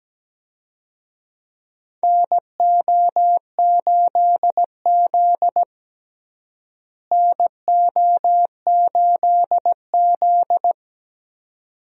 Morse code, heard as a keyed tone, about 17 wpm, 700 Hz. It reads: NO8Z NO8Z
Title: USA Call Signs - 4 Characters - Repeated 2x - 17wpm